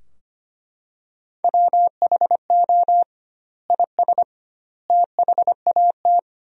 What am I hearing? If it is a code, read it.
Morse code, heard as a keyed tone, about 25 wpm, 700 Hz. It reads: WHO IS THAT
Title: 2x Word-Spacing - Sentences from Top 200 Words - 25wpm